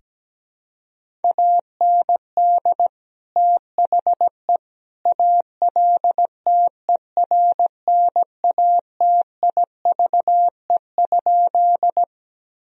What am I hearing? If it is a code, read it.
Morse code, heard as a keyed tone, about 17 wpm, 700 Hz. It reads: AND THE ALTERNATIVE?
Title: Sentences from Top 1000 Words - Review Part 4 - 17wpm